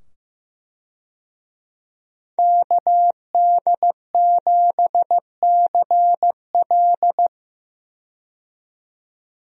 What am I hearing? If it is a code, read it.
Morse code, heard as a keyed tone, about 15 wpm, 700 Hz. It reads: KD7CL
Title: USA Call Signs - 5 Characters - 15wpm